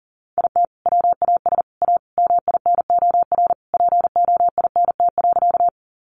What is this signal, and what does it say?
Morse code, heard as a keyed tone, about 40 wpm, 700 Hz. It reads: IT WAS A MINOR POINT.
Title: Sentences from Top 1500 Words 40wpm